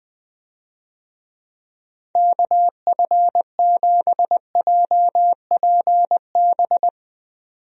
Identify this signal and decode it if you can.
Morse code, heard as a keyed tone, about 21 wpm, 700 Hz. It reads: KF7JPB
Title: USA Call Signs - 6 Characters - 20wpm